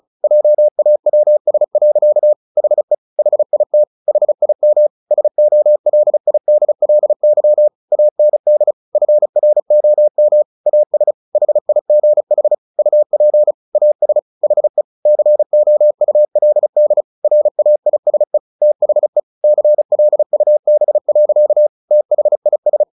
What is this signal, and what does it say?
Morse code, heard as a keyed tone, about 34 wpm, 600 Hz. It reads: JAWS. HE HIT HIM SOLIDLY AND FROM AS HIGH UP AS HE COULD RAISE THE CLUB. THIS
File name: the old man and the sea 2 35wpm